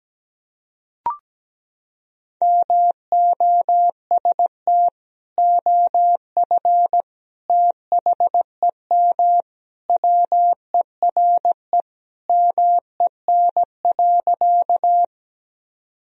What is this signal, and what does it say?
Morse code, heard as a keyed tone, about 17 wpm, 700 Hz. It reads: MOST OF THEM WERE MEN.
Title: Sentences from Top 1000 Words - Review Part 1 - 17wpm